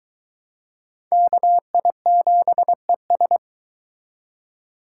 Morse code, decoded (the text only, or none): KI7ES